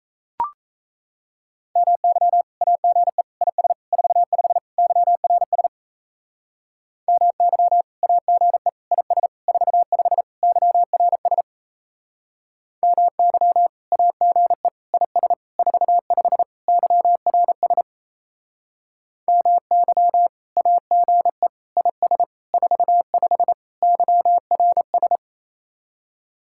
Morse code, heard as a keyed tone, about 36 wpm, 700 Hz. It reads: MY AGE IS 45 YRS MY AGE IS 45 YRS MY AGE IS 45 YRS MY AGE IS 45 YRS